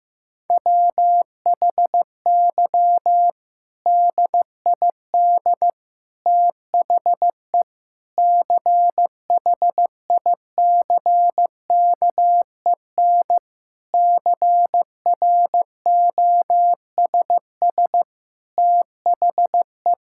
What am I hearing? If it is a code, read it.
Morse code, heard as a keyed tone, about 15 wpm, 700 Hz. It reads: WHY DID THE CHICKEN CROSS THE
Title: Sentences from Top 1000 Words - Review Part 5 - 15wpm